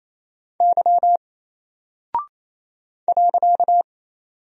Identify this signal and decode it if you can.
Morse code, heard as a keyed tone, about 27 wpm, 700 Hz. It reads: Y .
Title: Single Character (Letter:Number:Punctuation:Prosign) Alt 28wpm